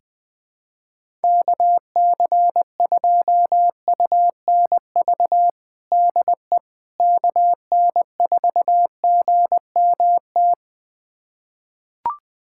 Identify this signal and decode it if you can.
Morse code, heard as a keyed tone, about 20 wpm, 700 Hz. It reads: KC2UNV DE KN4GMT
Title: USA Callsign DE Callsign - 6 Characters - 20wpm